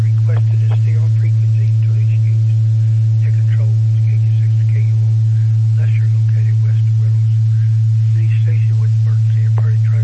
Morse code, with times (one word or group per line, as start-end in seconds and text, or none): none